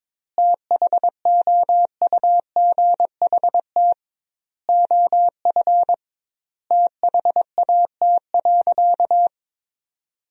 Morse code, heard as a keyed tone, about 22 wpm, 700 Hz. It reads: THOUGHT OF THAT.
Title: 2x Word-Spacing - Sentences from Top 300 Words - 22wpm